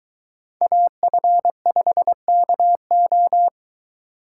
Morse code, heard as a keyed tone, about 23 wpm, 700 Hz. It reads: AF5KO